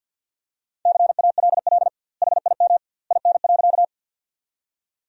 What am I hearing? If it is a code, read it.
Morse code, heard as a keyed tone, about 52 wpm, 700 Hz. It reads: CALL HIM IN.
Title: 2x Word-Spacing - Sentences from Top 100 Words - 50wpm